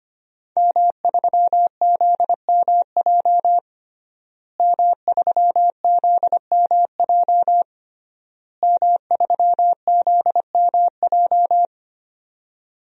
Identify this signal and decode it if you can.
Morse code, heard as a keyed tone, about 25 wpm, 700 Hz. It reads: M3ZMJ M3ZMJ M3ZMJ
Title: DXCC-100 Call Signs - 5 Characters - Repeated 3x - 25wpm